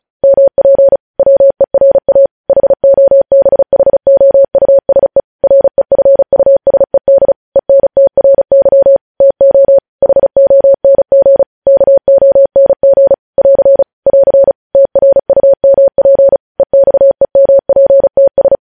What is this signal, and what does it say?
Morse code, heard as a keyed tone, about 36 wpm, 550 Hz. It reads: MP WERA HOBHOUSE REFUSED ENTRY TO HONG KONG <AR> <AR> TRUMP EXEMPTS
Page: News Headlines in Morse Code at 35 WPM